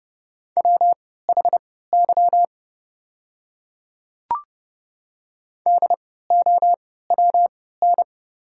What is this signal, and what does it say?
Morse code, heard as a keyed tone, about 30 wpm, spaced as for 17 wpm, 700 Hz. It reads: WHY DOWN
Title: 3x Character Spacing - Top 200 Words - 30wpm